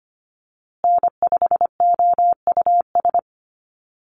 Morse code, heard as a keyed tone, about 25 wpm, 700 Hz. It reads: N5OUS
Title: USA Call Signs Repeated 2x 25wpm